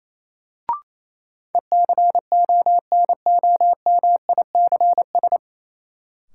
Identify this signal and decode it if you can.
Morse code, heard as a keyed tone, about 28 wpm, 700 Hz. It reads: ECONOMICS
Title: Sentences from Top 1000 Words - Review Part 3 - New Word Drill - 28wpm